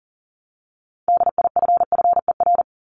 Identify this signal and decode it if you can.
Morse code, heard as a keyed tone, about 40 wpm, 700 Hz. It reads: DIFFER
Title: Top 500 Words 40wpm